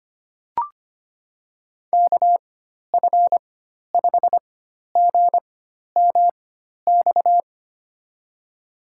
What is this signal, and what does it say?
Morse code, heard as a keyed tone, about 25 wpm, spaced as for 12 wpm, 700 Hz. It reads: KF5GMX
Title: USA Call Signs - 6 Characters - 4x Character Spacing - 25wpm